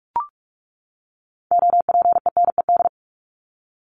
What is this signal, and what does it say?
Morse code, heard as a keyed tone, about 44 wpm, 700 Hz. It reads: OPENED
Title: ED Suffix - Top 100 Words with Base Words 45wpm